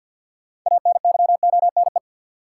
Morse code, heard as a keyed tone, about 53 wpm, 700 Hz. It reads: ANYONE
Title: Sentences from Top 900 Words - New Word Drill - 50wpm